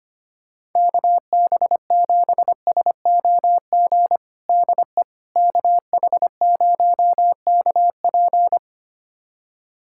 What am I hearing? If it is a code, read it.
Morse code, heard as a keyed tone, about 25 wpm, 700 Hz. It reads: KB7SOG DE KH0KP
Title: USA Callsign DE Callsign 25wpm